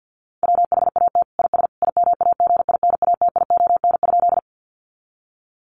Morse code, heard as a keyed tone, about 50 wpm, 700 Hz. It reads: WHAT IS IMAGINATION?
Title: Sentences from Top 1300 Words 50wpm